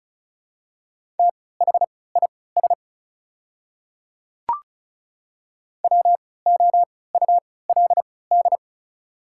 Morse code, heard as a keyed tone, about 36 wpm, spaced as for 20 wpm, 690 Hz. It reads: THIS WOULD
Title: 3x Character Spacing - Top 100 Words - 35wpm